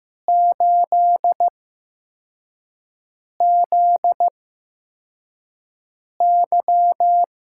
Morse code, heard as a keyed tone, about 15 wpm, spaced as for 4 wpm, 700 Hz. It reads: 8ZY
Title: USA Call Signs - 6 Characters - 8x Character Spacing - 15wpm